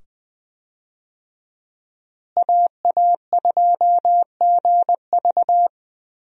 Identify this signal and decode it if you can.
Morse code, heard as a keyed tone, about 20 wpm, 700 Hz. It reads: AA2GV
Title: USA Call Signs - 5 Characters - Repeated 2x - 20wpm